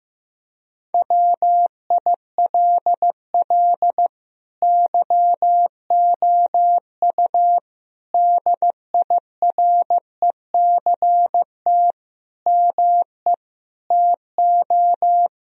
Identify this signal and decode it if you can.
Morse code, heard as a keyed tone, about 15 wpm, 700 Hz. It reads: WILL YOU DIRECT ME TO
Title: Sentences from Top 1000 Words - Review Part 2 - 15wpm